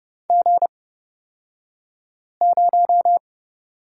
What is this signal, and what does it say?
Morse code, heard as a keyed tone, about 29 wpm, spaced as for 5 wpm, 700 Hz. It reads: G0
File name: Lesson 36d - BT - Two Characters 30wpm